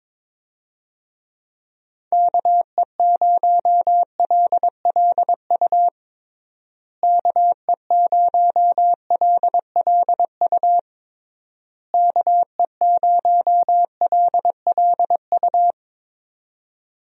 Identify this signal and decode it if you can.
Morse code, heard as a keyed tone, about 22 wpm, 700 Hz. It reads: KE0LLU KE0LLU KE0LLU
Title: USA Call Signs - 6 Characters - Repeated 3x - 22wpm